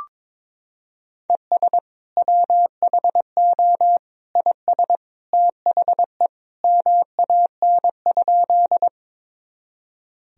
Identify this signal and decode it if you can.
Morse code, heard as a keyed tone, about 22 wpm, 700 Hz. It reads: ES WHO IS THE MAN?
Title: Sentences from Top 200 Words and Abbreviations 22wpm